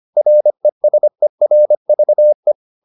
Morse code, heard as a keyed tone, about 25 wpm, 600 Hz. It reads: RESERVE
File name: the old man and the sea part 1 25wpm